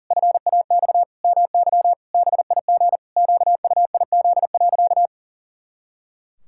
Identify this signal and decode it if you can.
Morse code, heard as a keyed tone, about 40 wpm, 700 Hz. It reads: FAX MY BIG QUIZ.
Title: Pangram Challenge v2 - 40wpm